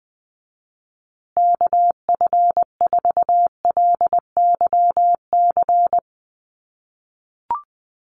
Morse code, heard as a keyed tone, about 20 wpm, 700 Hz. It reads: KF4LYC
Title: USA Call Signs Repeated 2x 20wpm